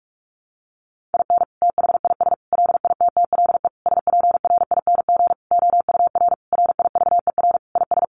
Morse code, heard as a keyed tone, about 44 wpm, 700 Hz. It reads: IN THIS LITTLE SPRING OUR RIVER IS
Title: Sentences from Top 1200 Words 45wpm